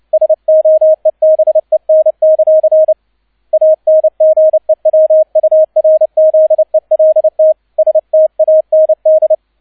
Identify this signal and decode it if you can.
Morse code, heard as a keyed tone, about 29 wpm, 630 Hz, with radio noise. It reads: SOEBEN; ANGEWURZELT STAND
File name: 002 Zweiter Streich 30wpm